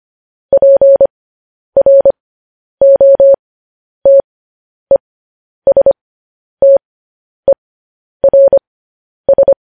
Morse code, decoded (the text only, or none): PROTESTERS